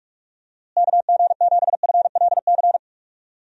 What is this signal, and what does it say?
Morse code, heard as a keyed tone, about 46 wpm, 700 Hz. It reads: KG7FLC
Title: USA Call Signs - 6 Characters - Repeated 2x - 45wpm